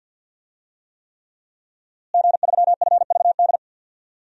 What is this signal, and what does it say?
Morse code, heard as a keyed tone, about 50 wpm, 700 Hz. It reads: G3FVD
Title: DXCC-100 Call Signs - 5 Characters - Repeated 3x - 50wpm